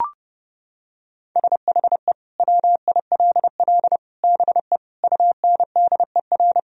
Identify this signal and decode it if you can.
Morse code, heard as a keyed tone, about 30 wpm, 700 Hz. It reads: SHE WILL BE UNDER
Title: Sentences from Top 1000 Words - Review Part 1 - 30wpm